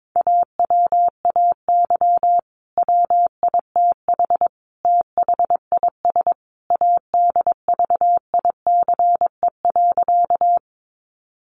Morse code, heard as a keyed tone, about 22 wpm, 700 Hz. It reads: AWAY WITH THIS ADVICE.
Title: Sentences from Top 1100 Words 22wpm